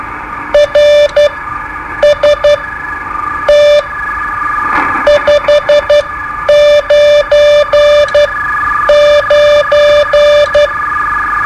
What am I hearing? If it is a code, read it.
Morse code, heard as a keyed tone, about 12 wpm, spaced as for 8 wpm, 590 Hz. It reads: RST 599
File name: qso-01